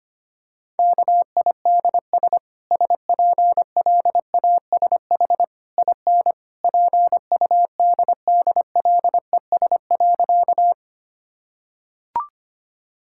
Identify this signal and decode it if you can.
Morse code, heard as a keyed tone, about 25 wpm, 700 Hz. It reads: KIDS SPLASH IN PUDDLES.
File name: Easy Sentences with Hints - 25wpm